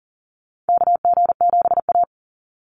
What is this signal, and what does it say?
Morse code, heard as a keyed tone, about 40 wpm, 700 Hz. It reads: KG7A
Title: Call Signs - 40wpm